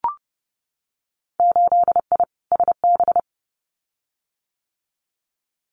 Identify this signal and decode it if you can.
Morse code, heard as a keyed tone, about 32 wpm, 700 Hz. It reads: 8I SB
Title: Winter Field Day Exchanges Encore 30wpm